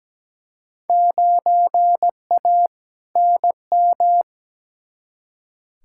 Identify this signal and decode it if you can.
Morse code, heard as a keyed tone, about 17 wpm, 700 Hz. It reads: 9A NM
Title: Field Day Exchanges Encore - 17wpm